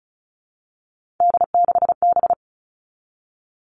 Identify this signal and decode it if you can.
Morse code, heard as a keyed tone, about 36 wpm, 700 Hz. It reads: D6B
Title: Three Challenging Letter-Number Combinations 35wpm